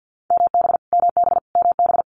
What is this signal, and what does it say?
Morse code, heard as a keyed tone, about 50 wpm, 700 Hz. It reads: MB MB MB
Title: Multi-letter Phonemes Repeated 3x 50wpm